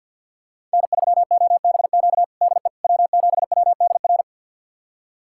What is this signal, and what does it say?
Morse code, heard as a keyed tone, about 47 wpm, 700 Hz. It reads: N3OBQ DE W7WDR
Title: USA Callsign DE Callsign - 5 Characters - 50wpm